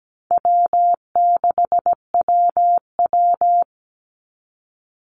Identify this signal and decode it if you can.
Morse code, heard as a keyed tone, about 17 wpm, 700 Hz. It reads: W6WW
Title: NA SOTA Chaser Callsigns 17wpm